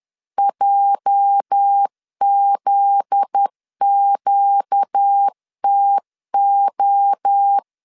Morse code, heard as a keyed tone, about 10 wpm, 800 Hz. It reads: JZQTO
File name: LEZIONI / 8 YJ45 / 8 YJ 10wpm 2651